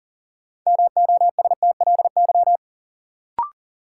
Morse code, heard as a keyed tone, about 40 wpm, 700 Hz. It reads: MOSTLY
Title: Sentences from Top 1300 Words - New Word Drill Alt - 40wpm